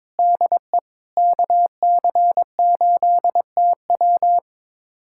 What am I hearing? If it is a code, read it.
Morse code, heard as a keyed tone, about 22 wpm, 700 Hz. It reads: DE KC8TW
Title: USA Callsign DE Callsign - 5 Characters - 22wpm